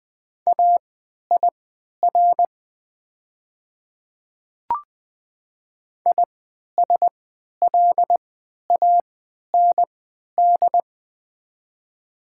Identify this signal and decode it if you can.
Morse code, heard as a keyed tone, about 21 wpm, spaced as for 12 wpm, 700 Hz. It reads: AIR ISLAND
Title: 3x Character Spacing - Top 500 Words - 20wpm